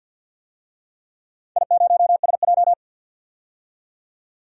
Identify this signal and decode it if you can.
Morse code, heard as a keyed tone, about 47 wpm, 700 Hz. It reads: I0SJ
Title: DXCC-100 Call Signs - 4 Characters - 50wpm